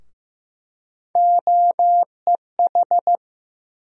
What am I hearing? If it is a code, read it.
Morse code, heard as a keyed tone, about 15 wpm, 700 Hz. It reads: OEH
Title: USA Call Sign Postfixes - 3 Letters 15wpm